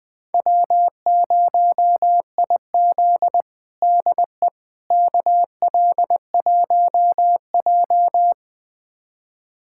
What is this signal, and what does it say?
Morse code, heard as a keyed tone, about 21 wpm, 700 Hz. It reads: W0IZ DE KL1J
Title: USA Callsign DE Callsign - 4 Characters - 20wpm